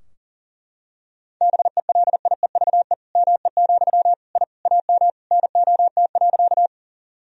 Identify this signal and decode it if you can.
Morse code, heard as a keyed tone, about 40 wpm, 700 Hz. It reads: BELIEVE ME, I AM NOT.
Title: Sentences from Top 1000 Words - Review Part 2 - 40wpm